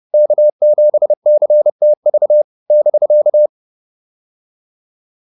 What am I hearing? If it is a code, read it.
Morse code, heard as a keyed tone, about 30 wpm, 600 Hz. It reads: K7CTV <BK>
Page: advanced-practice-files